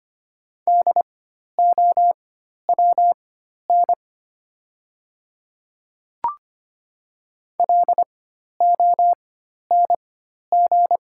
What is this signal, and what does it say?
Morse code, heard as a keyed tone, about 25 wpm, spaced as for 12 wpm, 700 Hz. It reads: DOWN LONG
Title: 4x Character Spacing - Top 100 Words - 25wpm